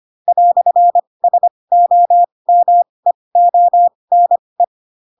Morse code, heard as a keyed tone, about 25 wpm, spaced as for 20 wpm, 700 Hz. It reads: "SOMEONE
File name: Pinocchio / Pinocchio ch6 15 25 ik1hge